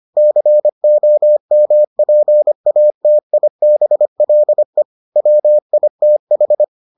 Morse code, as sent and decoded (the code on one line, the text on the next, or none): -.-. --- -- .--. .- - .. -... .-.. / .-- .. - ....
COMPATIBLE WITH